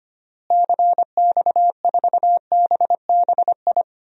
Morse code, heard as a keyed tone, about 25 wpm, 700 Hz. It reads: CX4BBI
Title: DXCC-100 Call Signs - 6 Characters - Repeated 2x - 25wpm